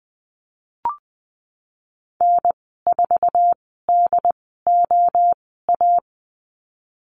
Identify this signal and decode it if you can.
Morse code, heard as a keyed tone, about 20 wpm, spaced as for 15 wpm, 700 Hz. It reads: N4DOA
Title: 069 Lesson 12g - Letter d - Call Signs 20wpm